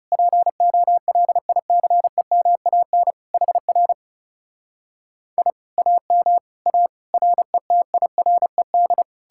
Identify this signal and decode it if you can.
Morse code, heard as a keyed tone, about 40 wpm, 700 Hz. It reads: POLICEMAN HR I AM A RETIRED